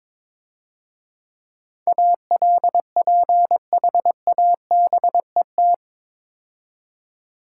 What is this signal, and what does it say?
Morse code, heard as a keyed tone, about 22 wpm, 690 Hz. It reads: ALPHABET
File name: Easy Words - Speed-Racing with Hints - 22wpm